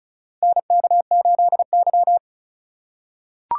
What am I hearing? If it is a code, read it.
Morse code, heard as a keyed tone, about 34 wpm, 700 Hz. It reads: NK8Y
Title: USA Call Signs - 4 Characters - 6x Character Spacing - 35wpm